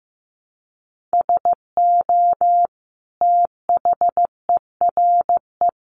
Morse code, heard as a keyed tone, about 15 wpm, 700 Hz. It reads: SO THERE
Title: Sentences from Top 100 Words 15wpm